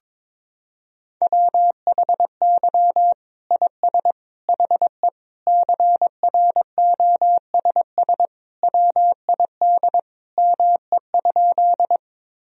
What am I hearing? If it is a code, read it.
Morse code, heard as a keyed tone, about 22 wpm, 700 Hz. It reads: WHY IS HE CROSS WID ME?
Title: Sentences from Top 300 Words and Abbreviations 22wpm